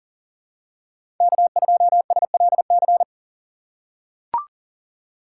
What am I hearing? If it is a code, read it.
Morse code, heard as a keyed tone, about 40 wpm, 700 Hz. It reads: K2SLC
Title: USA Call Signs - 5 Characters - 3x Character Spacing - 40wpm